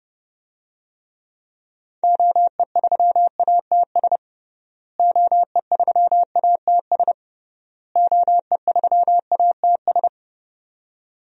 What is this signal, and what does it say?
Morse code, heard as a keyed tone, about 30 wpm, 700 Hz. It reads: OE3ATS OE3ATS OE3ATS
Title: DXCC-100 Call Signs - 6 Characters - Repeated 3x - 30wpm